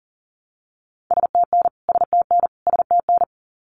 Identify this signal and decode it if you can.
Morse code, heard as a keyed tone, about 40 wpm, 700 Hz. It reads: STN STN STN
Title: Abbreviations Repeated 3x - 40wpm